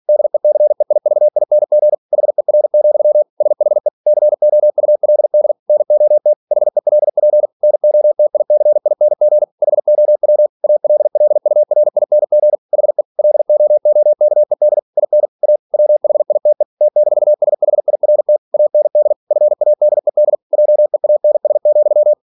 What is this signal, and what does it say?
Morse code, heard as a keyed tone, about 46 wpm, 600 Hz. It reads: DECEIVING HER, SHE COULD NOT HELP NOTICING HOW ALLURING HE LOOKED IN A WHITE T-SHIRT AND FADED JEANS,
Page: Never 47wpm